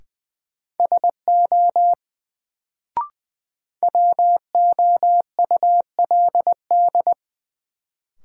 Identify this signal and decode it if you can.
Morse code, heard as a keyed tone, about 21 wpm, 700 Hz. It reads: SO WOULD
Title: Sets of 2 Words - New Word Drill - 20wpm